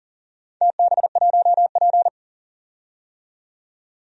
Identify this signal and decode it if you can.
Morse code, heard as a keyed tone, about 42 wpm, 700 Hz. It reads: TB1P
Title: DXCC-100 Call Signs - 4 Characters - 40wpm